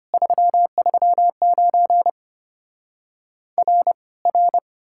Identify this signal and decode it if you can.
Morse code, heard as a keyed tone, about 33 wpm, 700 Hz. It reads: 339 R R